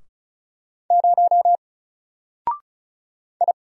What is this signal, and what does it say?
Morse code, heard as a keyed tone, about 35 wpm, 700 Hz. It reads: 0 I
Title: Single Character (Letter:Number:Punctuation:Prosign) 35wpm